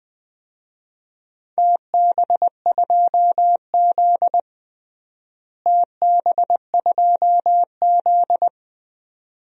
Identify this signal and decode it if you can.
Morse code, heard as a keyed tone, about 20 wpm, 700 Hz. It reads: TB2Z TB2Z
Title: DXCC-100 Call Signs - 4 Characters - Repeated 2x - 20wpm